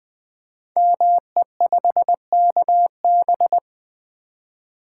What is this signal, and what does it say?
Morse code, heard as a keyed tone, about 20 wpm, 700 Hz. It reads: ME5KB
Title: Code Groups - 5 Letter:Numbers - 20wpm